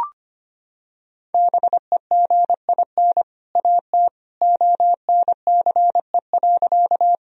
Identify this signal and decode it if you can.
Morse code, heard as a keyed tone, about 25 wpm, 700 Hz. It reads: BEGIN AT ONCE.
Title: Sentences from Top 1000 Words - Review Part 1 - 25wpm